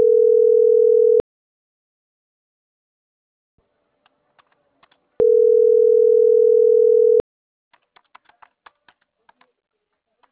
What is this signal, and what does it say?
Ringback tone (North America), 440 + 480 Hz.